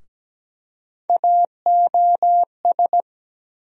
Morse code, heard as a keyed tone, about 17 wpm, 700 Hz. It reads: AOS
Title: USA Call Sign Postfixes - 3 Letters 17wpm